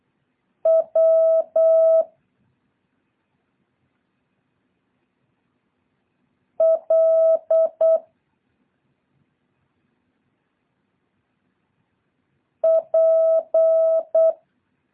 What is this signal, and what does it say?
Morse code, heard as a keyed tone, about 8 wpm, 650 Hz. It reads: W L P
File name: jh8cbh / 2023cw